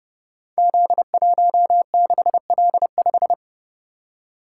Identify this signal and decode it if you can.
Morse code, heard as a keyed tone, about 28 wpm, 700 Hz. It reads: Z16L5
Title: Code Groups - 5 Letter:Numbers - 30wpm